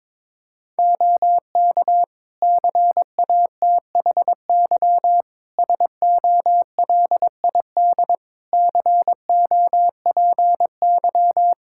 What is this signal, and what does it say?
Morse code, heard as a keyed tone, about 22 wpm, 700 Hz. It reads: OK CATHY SOLID COPY
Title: K1USN SST QSO Practice 22wpm